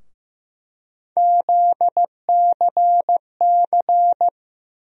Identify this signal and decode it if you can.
Morse code, heard as a keyed tone, about 15 wpm, 700 Hz. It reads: ZCC
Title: USA Call Sign Postfixes - 3 Letters 15wpm